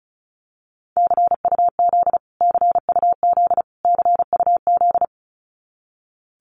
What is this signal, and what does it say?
Morse code, heard as a keyed tone, about 34 wpm, 700 Hz. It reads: CUZ CUZ CUZ
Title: Abbreviations Repeated 3x - 35wpm